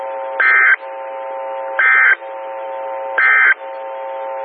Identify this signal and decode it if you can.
Emergency Alert System (SAME) tones, heard as harsh buzzing bursts. It is heard from 0.3 to 3.6 s.
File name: WX9MCS 147255 MHz SKYWARN Repeater